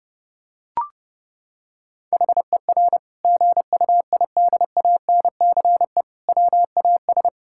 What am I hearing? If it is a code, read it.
Morse code, heard as a keyed tone, about 32 wpm, 700 Hz. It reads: HER GUIDANCE WAS
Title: Sentences from Top 1600 Words 30wpm